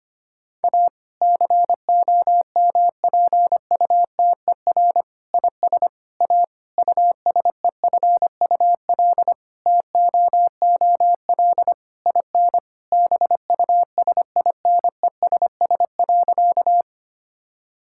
Morse code, heard as a keyed tone, about 25 wpm, 700 Hz. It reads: A COMPUTER IS A USEFUL TOOL IN BUSINESS.
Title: Sentences from Top 1000 Words - Review Part 6 - 25wpm